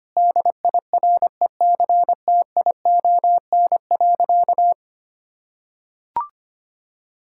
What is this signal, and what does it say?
Morse code, heard as a keyed tone, about 25 wpm, 700 Hz. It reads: DIRECTION.
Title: Sentences from Top 1000 Words - Review Part 5 - 25wpm